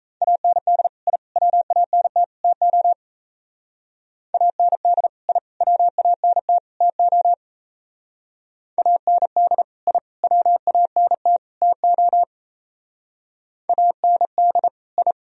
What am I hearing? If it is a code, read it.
Morse code, heard as a keyed tone, about 37 wpm, 700 Hz. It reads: AND I WANT TO AND I WANT TO AND I WANT TO AND I